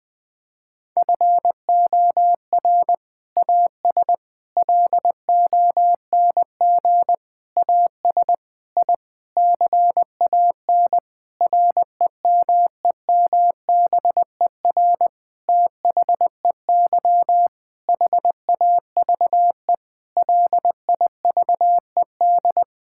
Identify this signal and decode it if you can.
Morse code, heard as a keyed tone, about 21 wpm, 700 Hz. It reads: FOR AS LONG AS I CAN REMEMBER THEY HAVE LIVED
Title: Sentences with 7-Word Phrases - Spoken Beginning - 20wpm